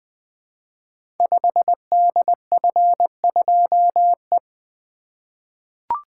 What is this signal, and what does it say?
Morse code, heard as a keyed tone, about 20 wpm, 700 Hz. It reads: 5DF2E
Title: Code Groups - 5 Letter:Numbers - 20wpm